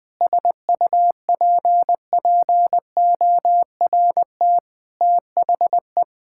Morse code, heard as a keyed tone, about 20 wpm, 700 Hz. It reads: SUPPORT THE
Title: Difficult Sentences with Hints - 20wpm